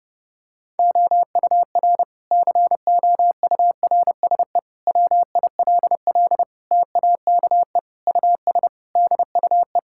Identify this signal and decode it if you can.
Morse code, heard as a keyed tone, about 30 wpm, 700 Hz. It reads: OUR COURSE WILL TAKE US DUE
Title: Sentences from Top 1000 Words - Review Part 6 - 30wpm